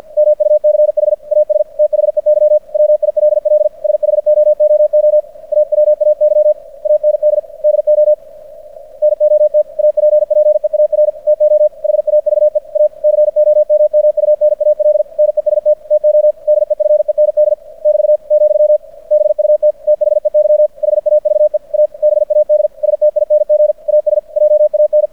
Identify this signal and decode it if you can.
Morse code, heard as a keyed tone, about 50 wpm, 600 Hz. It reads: MUCH AS THEY WILL SHOOO AWAY AND DO NOT APPEAR TO HAVE A COMMUNAL NEST TO DEFEND <BT> , BUT THEY HAVE A BAD STING AS JAN
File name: archived / CRSnet